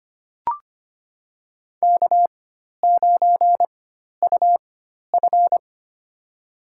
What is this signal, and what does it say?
Morse code, heard as a keyed tone, about 25 wpm, spaced as for 12 wpm, 690 Hz. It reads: K9UF
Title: USA Call Signs - 4 Characters - 4x Character Spacing - 25wpm